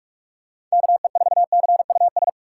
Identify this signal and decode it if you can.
Morse code, heard as a keyed tone, about 42 wpm, 700 Hz. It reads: KE4CUS
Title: USA Call Signs - 6 Characters - 2x Character Spacing - 45wpm